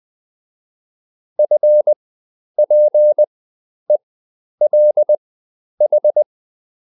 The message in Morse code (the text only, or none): FPELH